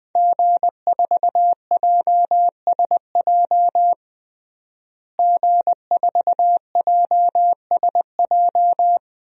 Morse code, heard as a keyed tone, about 20 wpm, 700 Hz. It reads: G4JSJ G4JSJ
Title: DXCC-100 Call Signs - 5 Characters - Repeated 2x - 20wpm